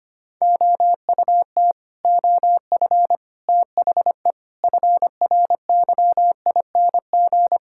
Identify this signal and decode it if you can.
Morse code, heard as a keyed tone, about 25 wpm, 700 Hz. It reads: OUT OF THE FRYING
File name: Sets of 9 Words - 25wpm